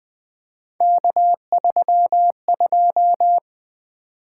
Morse code, heard as a keyed tone, about 20 wpm, 700 Hz. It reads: K32
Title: Three Letter-Number Combinations - 20wpm